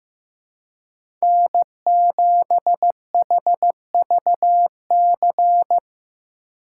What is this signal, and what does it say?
Morse code, heard as a keyed tone, about 15 wpm, 700 Hz. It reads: N7HVC
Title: USA Call Signs - 2x Character Spacing - 15wpm